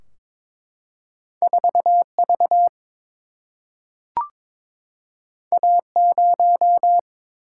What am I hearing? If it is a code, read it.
Morse code, heard as a keyed tone, about 22 wpm, 700 Hz. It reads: 4 V A 0
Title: Single-Letter Single-Number 22wpm